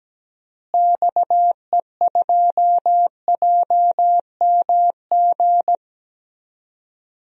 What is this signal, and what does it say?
Morse code, heard as a keyed tone, about 17 wpm, 700 Hz. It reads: XE2JMG
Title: DXCC-100 Call Signs - 6 Characters - Repeated 2x - 17wpm